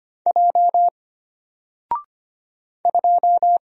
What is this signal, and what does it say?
Morse code, heard as a keyed tone, about 25 wpm, 700 Hz. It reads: J 2
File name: Single-Letter-Number-25wpm